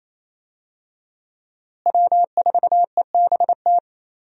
Morse code, heard as a keyed tone, about 27 wpm, 700 Hz. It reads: W4EBT
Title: USA Call Signs - 5 Characters - Repeated 3x - 28wpm